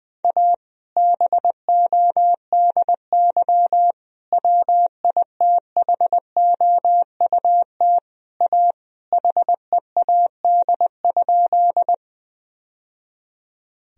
Morse code, heard as a keyed tone, about 20 wpm, 700 Hz. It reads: A BODY WITHOUT A HEAD?
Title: Sentences from Top 1000 Words - Review Part 3 - 20wpm